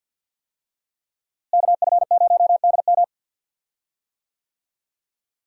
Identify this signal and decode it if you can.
Morse code, heard as a keyed tone, about 47 wpm, 700 Hz. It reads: KF0DM
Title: USA Call Signs - 5 Characters - Repeated 2x - 50wpm